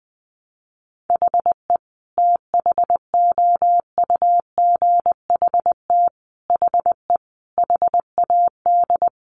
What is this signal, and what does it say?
Morse code, heard as a keyed tone, about 20 wpm, 700 Hz. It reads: HE THOUGHT HE HAD
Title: Sentences from Top 600 Words 20wpm